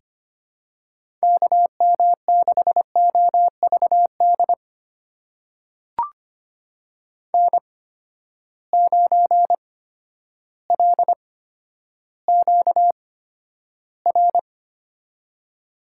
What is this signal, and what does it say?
Morse code, heard as a keyed tone, about 25 wpm, spaced as for 11 wpm, 700 Hz. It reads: KM6OVD N9LQR
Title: USA Call Signs - 8x Character Spacing - 25wpm